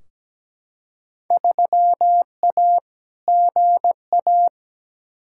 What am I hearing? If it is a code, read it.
Morse code, heard as a keyed tone, about 17 wpm, 700 Hz. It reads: 3A GA
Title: Field Day Exchanges - 17wpm